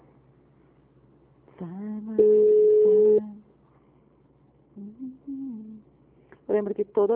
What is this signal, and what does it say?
Ringback tone (Europe, 425 Hz), with speech.